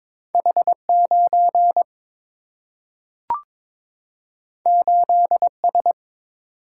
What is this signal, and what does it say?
Morse code, heard as a keyed tone, about 22 wpm, 700 Hz. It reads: H9 8S
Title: Single-Letter Single-Number 22wpm